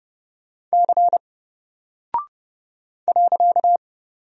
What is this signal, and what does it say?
Morse code, heard as a keyed tone, about 29 wpm, 700 Hz. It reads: C .
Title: Single Character (Letter:Number:Punctuation:Prosign) 30wpm